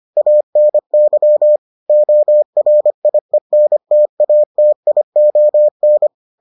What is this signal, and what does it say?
Morse code, heard as a keyed tone, about 25 wpm, 600 Hz. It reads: ANY ORIENTATION